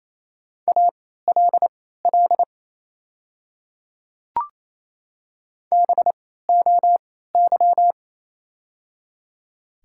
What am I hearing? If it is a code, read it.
Morse code, heard as a keyed tone, about 27 wpm, spaced as for 16 wpm, 700 Hz. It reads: ALL BOY
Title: 3x Character Spacing - Top 300 Words - 28wpm